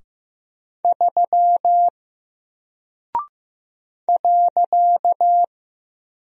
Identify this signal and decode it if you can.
Morse code, heard as a keyed tone, about 15 wpm, 700 Hz. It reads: 3 .
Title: Single Character (Letter:Number:Punctuation:Prosign) Alt 15wpm